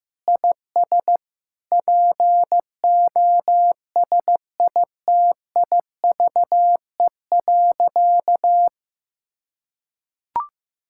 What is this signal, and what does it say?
Morse code, heard as a keyed tone, about 15 wpm, 700 Hz. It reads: IS POSITIVE.
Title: Sentences from Top 1000 Words - Review Part 5 - 15wpm